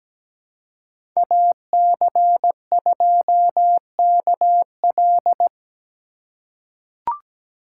Morse code, heard as a keyed tone, about 17 wpm, 700 Hz. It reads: AC2KL
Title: USA Call Signs - 5 Characters - 3x Character Spacing - 17wpm